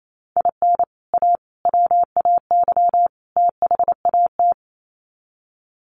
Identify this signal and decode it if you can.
Morse code, heard as a keyed tone, about 28 wpm, 700 Hz. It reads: IN A WAY THAT
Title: Sets of 4 Words 28wpm